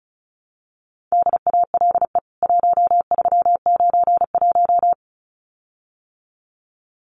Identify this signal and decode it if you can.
Morse code, heard as a keyed tone, about 34 wpm, 700 Hz. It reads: DALE 1391
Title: CWT Exchange 35wpm